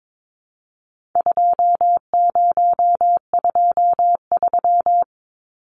Morse code, heard as a keyed tone, about 22 wpm, 700 Hz. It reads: 2023
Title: Numbers - 4-Digit 22wpm